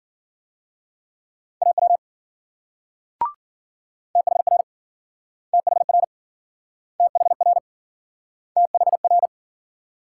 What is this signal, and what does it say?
Morse code, heard as a keyed tone, about 53 wpm, 700 Hz. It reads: AW THR THR THR THR